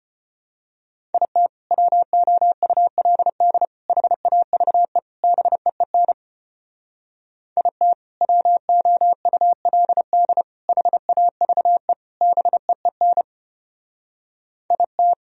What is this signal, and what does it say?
Morse code, heard as a keyed tone, about 31 wpm, 700 Hz. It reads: IT WOULD HAVE BEEN IT WOULD HAVE BEEN IT
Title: Speed-Racing - Sets of 4 Words 25wpm